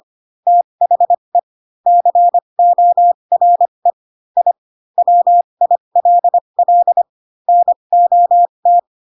Morse code, heard as a keyed tone, about 25 wpm, spaced as for 22 wpm, 700 Hz. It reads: THE CORE I WILL NOT
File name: Pinocchio / Pinocchio ch7 18 25 ik1hge